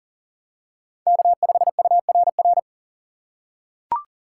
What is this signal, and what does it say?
Morse code, heard as a keyed tone, about 40 wpm, 700 Hz. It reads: K5URR
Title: USA Call Signs - 5 Characters - 2x Character Spacing - 40wpm